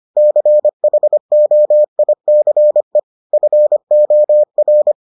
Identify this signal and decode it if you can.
Morse code, heard as a keyed tone, about 25 wpm, 600 Hz. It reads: CHOICE FOR